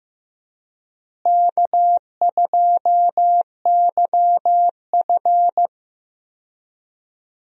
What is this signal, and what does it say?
Morse code, heard as a keyed tone, about 16 wpm, 700 Hz. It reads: K2YF
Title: USA Call Signs - 4 Characters - Repeated 2x - 15wpm